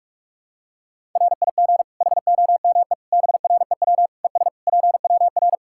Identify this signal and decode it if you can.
Morse code, heard as a keyed tone, about 46 wpm, 700 Hz. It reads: RIG HOME BREW ES PWR
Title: K1USN SST QSO Practice 45wpm